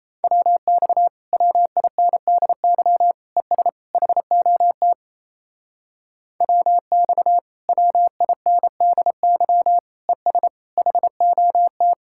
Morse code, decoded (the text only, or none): WX WINDY ES HOT WX WINDY ES HOT